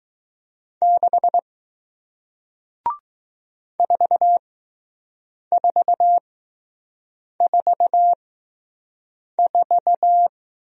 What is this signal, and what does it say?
Morse code, heard as a keyed tone, about 20 wpm, spaced as for 6 wpm, 700 Hz. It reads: 6 4444